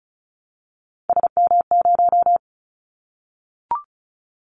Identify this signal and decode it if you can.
Morse code, heard as a keyed tone, about 36 wpm, 700 Hz. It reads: SM0
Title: Callsign Prefixes for Top 25 DXCC - 35wpm